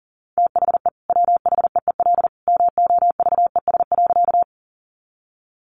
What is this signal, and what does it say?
Morse code, heard as a keyed tone, about 40 wpm, 700 Hz. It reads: THE WHEEL MOVES.
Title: Additional Easy Phrases 40wpm